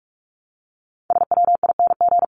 Morse code, heard as a keyed tone, about 44 wpm, 700 Hz. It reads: SWING
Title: Top 1751-2000 Words 45wpm